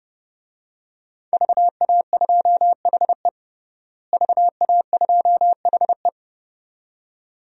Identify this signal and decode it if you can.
Morse code, heard as a keyed tone, about 30 wpm, 700 Hz. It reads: VA2HE VA2HE
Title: DXCC-100 Call Signs - 5 Characters - Repeated 2x - 30wpm